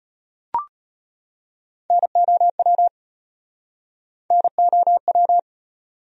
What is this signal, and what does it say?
Morse code, heard as a keyed tone, about 35 wpm, 700 Hz. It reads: NOW NOW